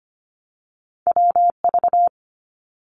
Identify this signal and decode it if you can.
Morse code, heard as a keyed tone, about 25 wpm, 700 Hz. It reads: WV
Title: US States Repeated 3x 25wpm